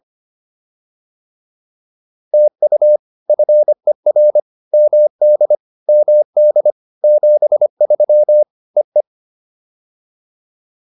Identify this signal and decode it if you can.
Morse code, heard as a keyed tone, about 25 wpm, 590 Hz. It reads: TU FER MD MD 73 EE